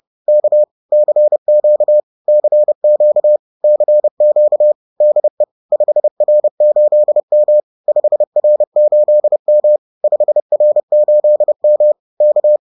Morse code, heard as a keyed tone, about 30 wpm, 600 Hz. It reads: K CQ CQ CQ DE 5R8M 5R8M 5R8M K